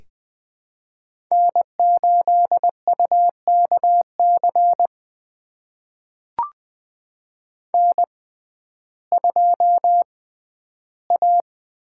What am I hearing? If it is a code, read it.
Morse code, heard as a keyed tone, about 20 wpm, spaced as for 12 wpm, 700 Hz. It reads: N8UKC N2A
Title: USA Call Signs - 5 Characters - 6x Character Spacing - 20wpm